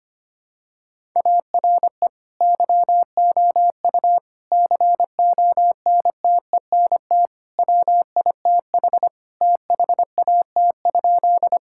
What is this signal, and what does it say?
Morse code, heard as a keyed tone, about 25 wpm, 700 Hz. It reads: ARE YOU CONTENT WITH THAT?
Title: Sentences from Top 1000 Words - Review Part 6 - 25wpm